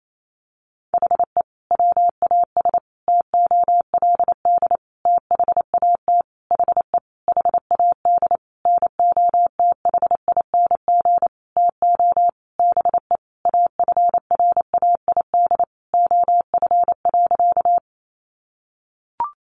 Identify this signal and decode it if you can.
Morse code, heard as a keyed tone, about 28 wpm, 700 Hz. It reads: HE WAS TOLD THAT HE HAD NOTHING TO BE AFRAID OF.
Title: Sentences from Top 1500 Words 28wpm